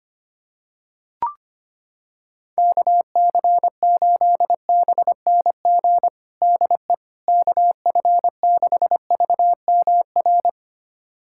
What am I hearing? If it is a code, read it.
Morse code, heard as a keyed tone, about 25 wpm, 700 Hz. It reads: KC8BNG DE KF6VMR
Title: USA Callsign DE Callsign - 6 Characters - 25wpm